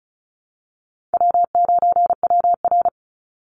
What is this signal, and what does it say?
Morse code, heard as a keyed tone, about 34 wpm, 700 Hz. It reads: W9WR
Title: Call Signs Repeated 2x - 35wpm